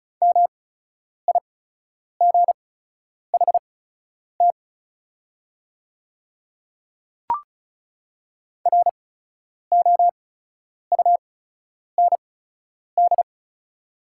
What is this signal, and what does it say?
Morse code, heard as a keyed tone, about 36 wpm, spaced as for 10 wpm, 700 Hz. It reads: MIGHT ROUND
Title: 8x Character Spacing - Top 300 Words - 35wpm